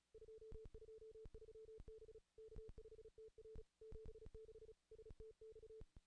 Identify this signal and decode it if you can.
Morse code, heard as a keyed tone, about 37 wpm, 440 Hz. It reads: 223 B K5TR 76 STX
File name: contests / ss-cw-k5tr-op-n5rz -2200